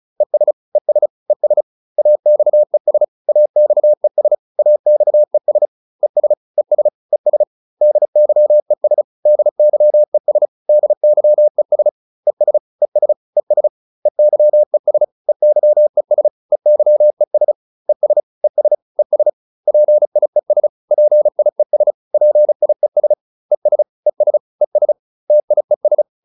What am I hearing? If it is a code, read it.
Morse code, heard as a keyed tone, about 34 wpm, 600 Hz. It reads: ES ES ES AXES AXES AXES ES ES ES DYES DYES DYES ES ES ES EYES EYES EYES ES ES ES PIES PIES PIES ES ES ES TIES